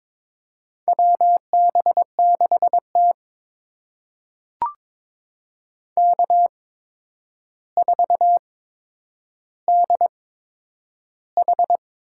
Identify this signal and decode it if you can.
Morse code, heard as a keyed tone, about 22 wpm, spaced as for 9 wpm, 700 Hz. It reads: WB6T K4DH
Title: USA Call Signs - 4 Characters - 8x Character Spacing - 22wpm